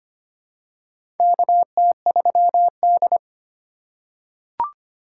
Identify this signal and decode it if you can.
Morse code, heard as a keyed tone, about 25 wpm, 700 Hz. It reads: KT3D
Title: USA Call Signs - 4 Characters - 6x Character Spacing - 25wpm